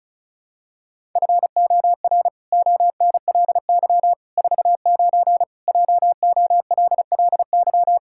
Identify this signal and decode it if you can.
Morse code, heard as a keyed tone, about 36 wpm, 700 Hz. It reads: FOR ONLY 49 JOLLY
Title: Pangram Challenge v2 - 35wpm